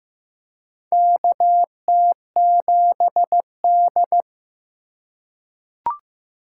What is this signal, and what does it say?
Morse code, heard as a keyed tone, about 16 wpm, 700 Hz. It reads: KT7D
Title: USA Call Signs - 4 Characters - 4x Character Spacing - 15wpm